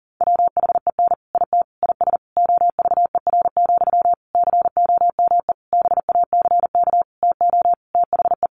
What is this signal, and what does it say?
Morse code, heard as a keyed tone, about 40 wpm, 700 Hz. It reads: WHEN IT IS OVER, COME BACK TO THE
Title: Sentences from Top 1200 Words 40wpm